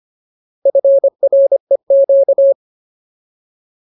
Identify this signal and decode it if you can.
Morse code, heard as a keyed tone, about 25 wpm, 550 Hz. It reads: FREQ
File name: Ultimate Word List / 25WPM Ultimate